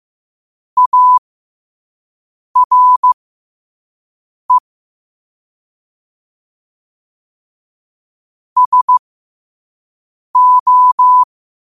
Morse code, heard as a keyed tone, about 15 wpm, spaced as for 5 wpm, 1000 Hz. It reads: ARE SO
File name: qotdfiles / 5wpm